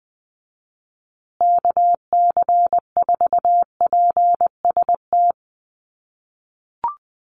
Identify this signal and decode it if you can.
Morse code, heard as a keyed tone, about 20 wpm, 700 Hz. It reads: KC4PST
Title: USA Call Signs Repeated 3x 20wpm